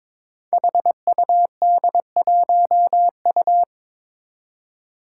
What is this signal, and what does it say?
Morse code, heard as a keyed tone, about 22 wpm, 700 Hz. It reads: HUD1U
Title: Five Challenging Letter-Number Combinations 22wpm